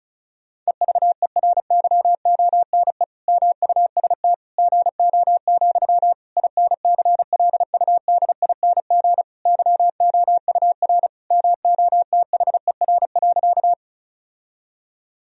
Morse code, heard as a keyed tone, about 36 wpm, 700 Hz. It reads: EVERYONE MUST GO, INCLUDING YOUR MOTHER.
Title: Sentences from Top 1000 Words - Review Part 4 - 35wpm